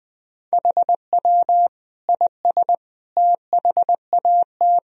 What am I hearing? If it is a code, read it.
Morse code, heard as a keyed tone, about 21 wpm, 700 Hz. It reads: HW IS THAT
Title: Sentences from Top 500 Words and Abbreviations 20wpm